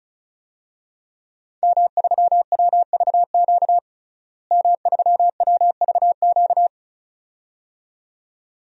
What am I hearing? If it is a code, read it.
Morse code, heard as a keyed tone, about 36 wpm, 700 Hz. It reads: M3WVQ M3WVQ
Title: DXCC-100 Call Signs - 5 Characters - Repeated 2x - 35wpm